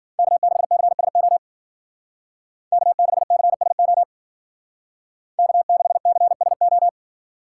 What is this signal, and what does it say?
Morse code, heard as a keyed tone, about 49 wpm, 700 Hz. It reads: K6CSO K6CSO K6CSO
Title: Speed Racing - USA 1x3 Call Signs 40wpm